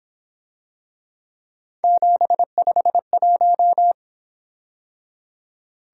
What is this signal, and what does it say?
Morse code, heard as a keyed tone, about 26 wpm, 700 Hz. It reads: 751
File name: Speed-Racing - Three Letter-Number Combinations - 22wpm